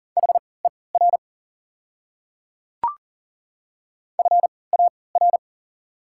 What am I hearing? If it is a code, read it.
Morse code, heard as a keyed tone, about 40 wpm, spaced as for 23 wpm, 700 Hz. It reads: HER FAR
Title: 3x Character Spacing - Top 500 Words - 40wpm